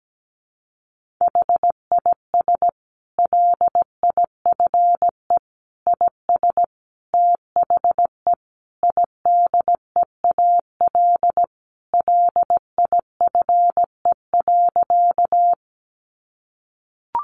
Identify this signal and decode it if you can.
Morse code, heard as a keyed tone, about 17 wpm, 700 Hz. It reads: HIS LIFE IS THE IDEAL LIFE.
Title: Sentences from Top 1400 Words 17wpm